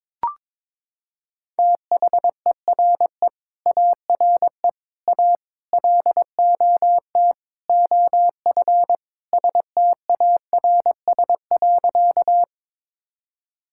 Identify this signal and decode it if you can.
Morse code, heard as a keyed tone, about 22 wpm, 700 Hz. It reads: THERE ARE A LOT OF STARS.
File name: practice / 22wpm / Sentences with 4-Word Phrases - 22wpm